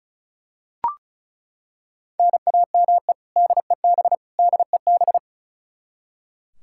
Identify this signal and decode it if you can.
Morse code, heard as a keyed tone, about 36 wpm, 700 Hz. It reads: NAME DEB DEB
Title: Practice QSO - Exchange 1 - Beginning - 35wpm